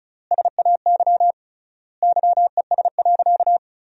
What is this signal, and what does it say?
Morse code, heard as a keyed tone, about 34 wpm, 700 Hz. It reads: SAY YES.
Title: 3x Word-Spacing - Sentences from Top 500 Words - 35wpm